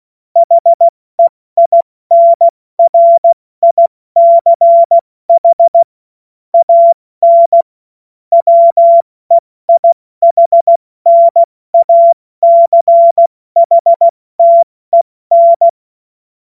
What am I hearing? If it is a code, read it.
Morse code, heard as a keyed tone, about 16 wpm, 680 Hz. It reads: HEINRICH AN WEIHNACHTEN
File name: Morsekurs / Witze / Witz 25. April 70 F80